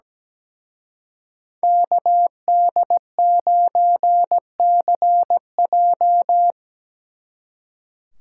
Morse code, heard as a keyed tone, about 20 wpm, 700 Hz. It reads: KD9CJ
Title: Speed Racing - USA Call Signs 17wpm